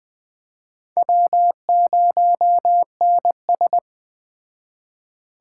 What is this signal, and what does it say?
Morse code, heard as a keyed tone, about 19 wpm, 700 Hz. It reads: W0NS